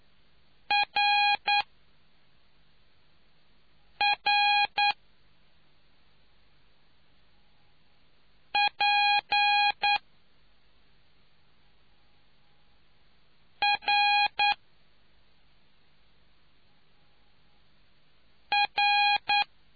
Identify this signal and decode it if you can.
Morse code, heard as a keyed tone, about 9 wpm, spaced as for 3 wpm, 820 Hz. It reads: RRP R R